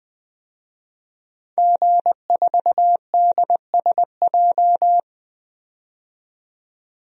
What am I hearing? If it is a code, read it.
Morse code, heard as a keyed tone, about 20 wpm, 700 Hz. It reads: G4DSJ
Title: DXCC-100 Call Signs - 5 Characters - 20wpm